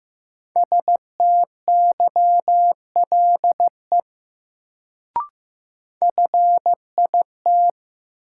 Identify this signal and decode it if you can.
Morse code, heard as a keyed tone, about 15 wpm, 700 Hz. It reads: STYLE FIT
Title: Sentences from Top 800 Words - New Word Drill - 15wpm